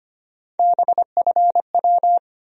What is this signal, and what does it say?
Morse code, heard as a keyed tone, about 25 wpm, 700 Hz. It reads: BFW
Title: USA Call Sign Postfixes - 3 Letters 25wpm